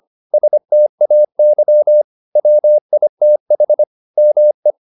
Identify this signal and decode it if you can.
Morse code, heard as a keyed tone, about 25 wpm, 600 Hz. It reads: STAY WITH ME
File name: the old man and the sea part 1 25wpm